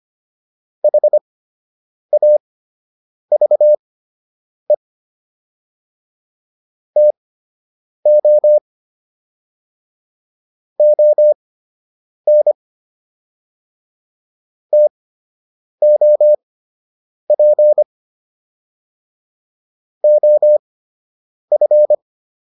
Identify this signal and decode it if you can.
Morse code, heard as a keyed tone, about 25 wpm, spaced as for 8 wpm, 600 Hz. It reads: HAVE TO ON TOP OF